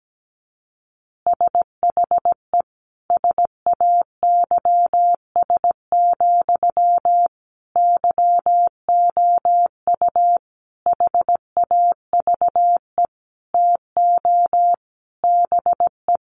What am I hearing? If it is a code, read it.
Morse code, heard as a keyed tone, about 17 wpm, 700 Hz. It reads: SHE SAYS, YOU HAVE TO BE
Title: Easy Phrases 17wpm